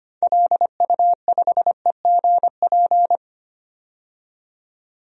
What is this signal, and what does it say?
Morse code, heard as a keyed tone, about 25 wpm, 700 Hz. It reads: LU5EGP
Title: DXCC-100 Call Signs - 6 Characters - 25wpm